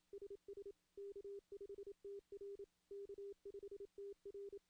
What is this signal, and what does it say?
Morse code, heard as a keyed tone, about 27 wpm, 400 Hz. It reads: SS K5TR K5TR
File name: CW / CW-SS-k5tr- -1700